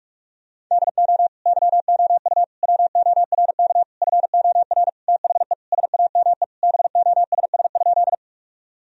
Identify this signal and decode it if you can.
Morse code, heard as a keyed tone, about 46 wpm, 700 Hz. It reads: DO YOU WORK FOR THE SAME BOSS?
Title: Sentences from Top 1000 Words - Review Part 4 - 45wpm